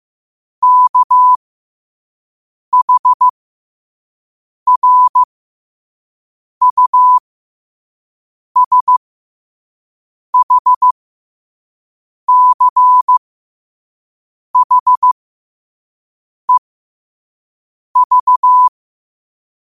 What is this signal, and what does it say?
Morse code, heard as a keyed tone, about 15 wpm, spaced as for 5 wpm, 1000 Hz. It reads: KHRUSHCHEV
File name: qotdfiles / 5wpm